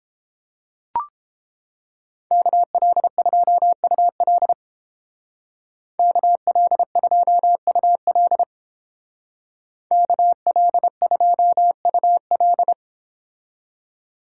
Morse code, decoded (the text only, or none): KL2UL KL2UL KL2UL